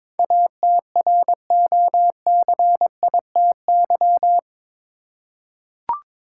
Morse code, heard as a keyed tone, about 22 wpm, 700 Hz. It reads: ATROCITY
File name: practice / 22wpm / Eight Letter Words - 22wpm